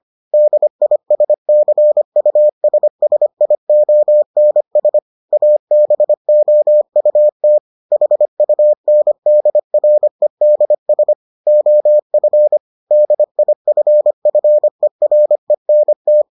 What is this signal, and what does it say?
Morse code, heard as a keyed tone, about 25 wpm, 600 Hz. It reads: DISCUSSIONS ABOUT HUNDREDS OF DIFFERENT